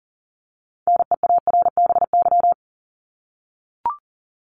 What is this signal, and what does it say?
Morse code, heard as a keyed tone, about 40 wpm, 700 Hz. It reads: NEARBY
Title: Top 1501-1750 Words 40wpm